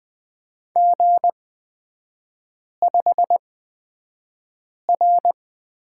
Morse code, heard as a keyed tone, about 20 wpm, spaced as for 5 wpm, 700 Hz. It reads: G5R
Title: 224 Lesson 36e - BT - Three Characters 20wpm